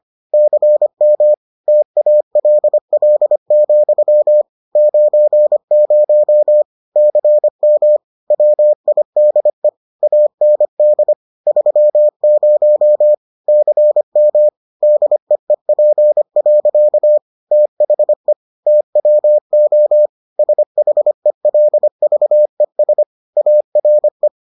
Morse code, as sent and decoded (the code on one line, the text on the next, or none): -.-. -- / - .- .-.. .-.. --..-- / ----. ----- / -.-. -- / .-- .. -.. / .- -. -.. / ...-- ----- / -.-. -- / -.. . .--. .-.-.- / - .... / - .-- --- / ... .... .-.. ...- . ... / .- .-. .
CM TALL, 90 CM WIDE AND 30 CM DEEP. THE TWO SHELVES ARE